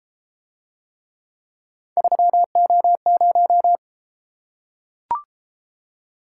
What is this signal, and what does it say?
Morse code, heard as a keyed tone, about 33 wpm, 700 Hz. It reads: 3O0